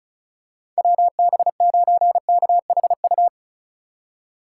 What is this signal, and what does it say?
Morse code, heard as a keyed tone, about 34 wpm, 700 Hz. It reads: WB9KHU
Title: USA Call Signs - 6 Characters - 4x Character Spacing - 35wpm